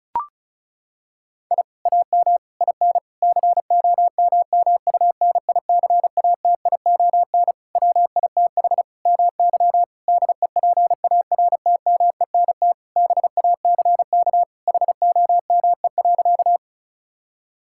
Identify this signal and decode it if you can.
Morse code, heard as a keyed tone, about 34 wpm, 700 Hz. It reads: I AM IN COMMUNICATION WITH MY DEPARTMENT BACK HOME.
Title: Sentences from Top 1000 Words - Review Part 6 - 35wpm